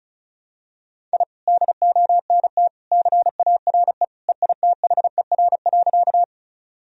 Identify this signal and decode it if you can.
Morse code, heard as a keyed tone, about 34 wpm, 700 Hz. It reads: I DONT CARE EITHER.
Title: Sentences from Top 1000 Words - Review Part 4 - 35wpm